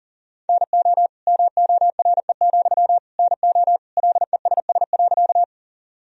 Morse code, heard as a keyed tone, about 40 wpm, 700 Hz. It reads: NO MORE, NO LESS.
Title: Sentences from Top 1000 Words - Review Part 2 - 40wpm